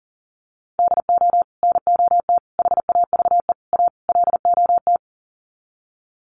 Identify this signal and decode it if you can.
Morse code, heard as a keyed tone, about 37 wpm, 700 Hz. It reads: DO NOT HAVE A LOT
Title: Sets of 5 Words Encore 40wpm